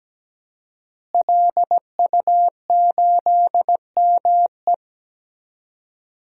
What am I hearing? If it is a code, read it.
Morse code, heard as a keyed tone, about 17 wpm, 700 Hz. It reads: LU8ME
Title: DXCC-100 Call Signs - 5 Characters - Repeated 2x - 17wpm